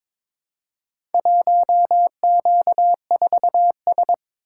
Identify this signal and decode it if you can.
Morse code, heard as a keyed tone, about 22 wpm, 700 Hz. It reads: 1Q4S
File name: practice / 22wpm / Four Letter-Number Combinations - 22wpm